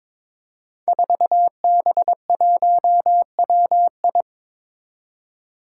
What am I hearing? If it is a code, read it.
Morse code, heard as a keyed tone, about 22 wpm, 700 Hz. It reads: 4B1WI
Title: Five Challenging Letter-Number Combinations 22wpm